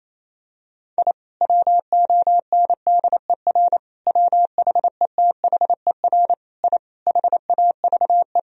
Morse code, heard as a keyed tone, about 28 wpm, 700 Hz. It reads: I WONDER WHETHER I HAVE
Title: Sentences from Top 1000 Words - Review Part 3 - 28wpm